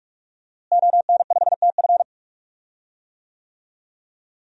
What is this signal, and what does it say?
Morse code, heard as a keyed tone, about 43 wpm, 700 Hz. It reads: ON5TF
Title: DXCC-100 Call Signs - 5 Characters - Repeated 2x - 45wpm